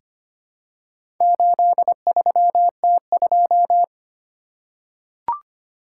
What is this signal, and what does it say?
Morse code, heard as a keyed tone, about 25 wpm, 700 Hz. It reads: 83T2
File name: practice / 25wpm / Four Letter-Number Combinations - 25wpm